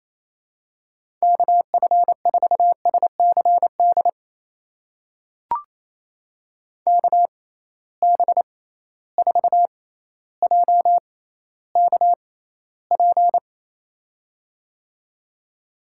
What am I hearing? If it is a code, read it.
Morse code, heard as a keyed tone, about 28 wpm, spaced as for 14 wpm, 700 Hz. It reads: KF4SCD KB4JKP
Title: USA Call Signs - 6 Characters - 6x Character Spacing - 28wpm